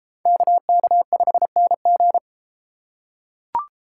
Morse code, heard as a keyed tone, about 32 wpm, 700 Hz. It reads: KK5NG